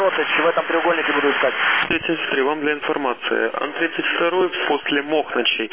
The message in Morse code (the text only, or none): none